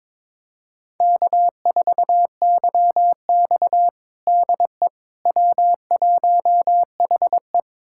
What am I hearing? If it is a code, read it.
Morse code, heard as a keyed tone, about 22 wpm, 700 Hz. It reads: K4YX DE W1HE
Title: USA Callsign DE Callsign - 4 Characters - 22wpm